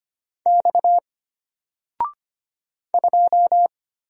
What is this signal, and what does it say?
Morse code, heard as a keyed tone, about 25 wpm, 700 Hz. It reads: X 2
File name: Single-Letter-Number-25wpm